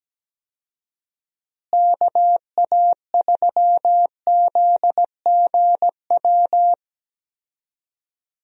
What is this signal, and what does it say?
Morse code, heard as a keyed tone, about 17 wpm, 700 Hz. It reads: KA3ZGW
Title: USA Call Signs - 6 Characters - Repeated 3x - 17wpm